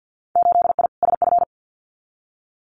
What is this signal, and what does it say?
Morse code, heard as a keyed tone, about 50 wpm, 700 Hz. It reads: 8I SF
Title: Winter Field Day Exchanges Encore 50wpm